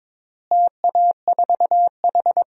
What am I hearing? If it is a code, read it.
Morse code, heard as a keyed tone, about 22 wpm, 700 Hz. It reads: TA4H
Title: DXCC-100 Call Signs - 4 Characters - Repeated 3x - 22wpm